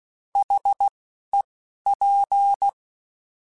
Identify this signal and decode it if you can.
Morse code, heard as a keyed tone, about 16 wpm, spaced as for 11 wpm, 800 Hz. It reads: HEP
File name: Alphabetical / Alphabetical mp3s / Hs @10